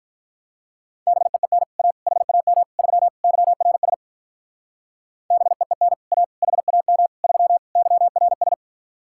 Morse code, heard as a keyed tone, about 45 wpm, 700 Hz. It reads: BEEN A HAM 3 YRS BEEN A HAM 3 YRS